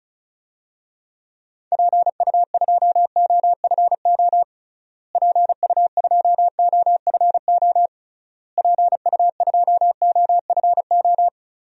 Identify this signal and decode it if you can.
Morse code, heard as a keyed tone, about 36 wpm, 700 Hz. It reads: PU2OFO PU2OFO PU2OFO
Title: DXCC-100 Call Signs - 6 Characters - Repeated 3x - 35wpm